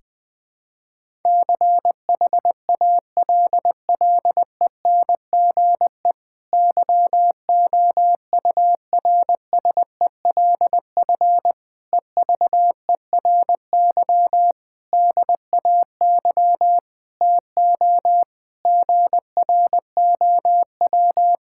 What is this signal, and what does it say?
Morse code, heard as a keyed tone, about 20 wpm, 700 Hz. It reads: CHALLENGE YOURSELF EVERY DAY TO GROW
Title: Difficult Sentences with Hints - 20wpm